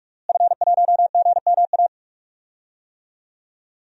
Morse code, heard as a keyed tone, about 45 wpm, 700 Hz. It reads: F1GMA
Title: DXCC-100 Call Signs - 5 Characters - Repeated 3x - 45wpm